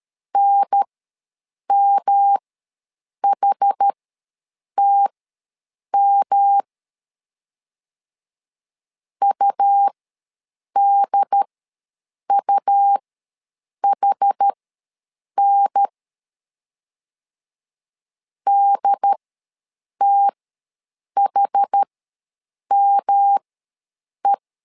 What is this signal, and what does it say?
Morse code, heard as a keyed tone, about 13 wpm, spaced as for 7 wpm, 800 Hz. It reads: NMHTM UDUHN DTHME
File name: LEZIONI / 3 MWRO / 3 M ws 12wpm 1745